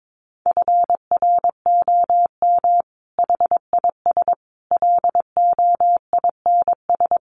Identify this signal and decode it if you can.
Morse code, heard as a keyed tone, about 22 wpm, 700 Hz. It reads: FROM HIS LOINS
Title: A Sound Of Thunder 22wpm